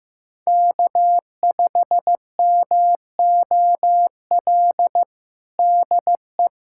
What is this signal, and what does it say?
Morse code, heard as a keyed tone, about 15 wpm, 700 Hz. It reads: K5MOL DE
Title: USA Callsign DE Callsign - 5 Characters - 15wpm